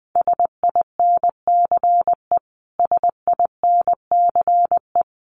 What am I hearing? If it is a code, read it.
Morse code, heard as a keyed tone, about 20 wpm, 700 Hz. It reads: SINCE SINCE
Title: Top 300 Words Repeated 20wpm